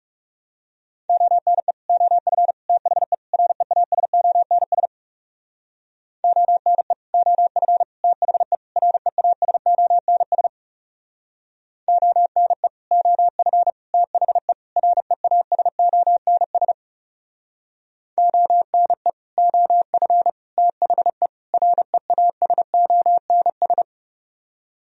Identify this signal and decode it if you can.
Morse code, heard as a keyed tone, about 40 wpm, 700 Hz. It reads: ONE OF THE REASONS ONE OF THE REASONS ONE OF THE REASONS ONE OF THE REASONS